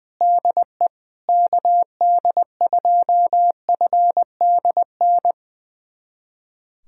Morse code, heard as a keyed tone, about 20 wpm, 700 Hz. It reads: DE KD2FDN
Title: USA Callsign DE Callsign 20wpm